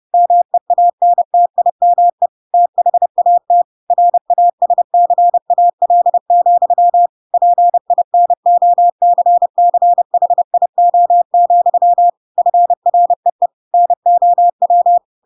Morse code, heard as a keyed tone, about 30 wpm, 700 Hz. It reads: MEANTIME THAT RASCAL, PINOCCHIO, FREE NOW